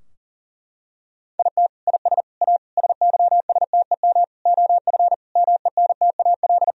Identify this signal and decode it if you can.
Morse code, heard as a keyed tone, about 40 wpm, 700 Hz. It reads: IT IS A SYSTEM OF MENTAL
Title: Sentences from Top 1000 Words - Review Part 4 - 40wpm